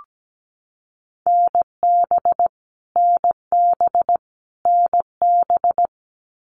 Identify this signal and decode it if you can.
Morse code, heard as a keyed tone, about 17 wpm, 700 Hz. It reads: NB NB NB
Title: Canadian Provinces and Territories - Repeated 3x - 17wpm